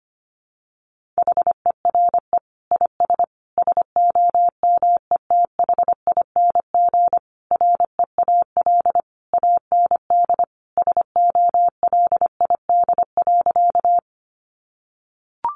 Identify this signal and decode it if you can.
Morse code, heard as a keyed tone, about 25 wpm, 700 Hz. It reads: HERE IS SOMETHING REAL AND SOLID.
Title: Sentences from Top 1500 Words 25wpm